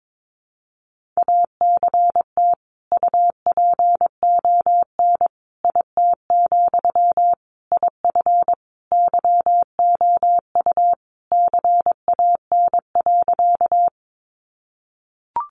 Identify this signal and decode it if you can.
Morse code, heard as a keyed tone, about 22 wpm, 700 Hz. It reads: ACT UPON IT, IF YOU CAN.
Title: Sentences from Top 1100 Words 22wpm